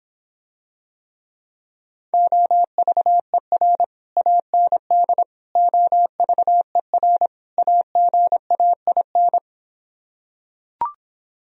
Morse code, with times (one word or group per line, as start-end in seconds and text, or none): 2.1-3.8 OVER
4.2-5.2 AND
5.6-7.3 OVER
7.6-9.4 AGAIN